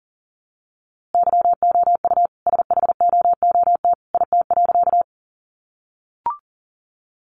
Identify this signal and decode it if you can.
Morse code, heard as a keyed tone, about 40 wpm, 700 Hz. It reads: YOU SHOOT IT.
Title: A Sound Of Thunder 40wpm